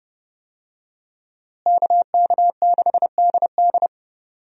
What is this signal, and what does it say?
Morse code, heard as a keyed tone, about 30 wpm, 700 Hz. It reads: KK6DD
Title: USA Call Signs - 5 Characters - Repeated 2x - 30wpm